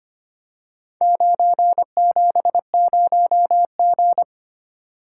Morse code, heard as a keyed tone, about 25 wpm, 700 Hz. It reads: 970G
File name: practice / 25wpm / Four Letter-Number Combinations - 25wpm